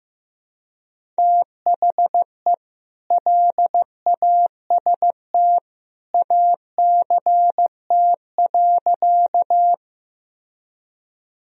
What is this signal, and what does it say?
Morse code, heard as a keyed tone, about 15 wpm, 700 Hz. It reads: THE LAST ACT.
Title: Sentences from Top 1000 Words - Review Part 1 - 15wpm